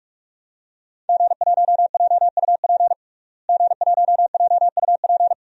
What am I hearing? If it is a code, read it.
Morse code, heard as a keyed tone, about 46 wpm, 700 Hz. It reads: G1JUP G1JUP
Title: DXCC-100 Call Signs - 5 Characters - Repeated 3x - 45wpm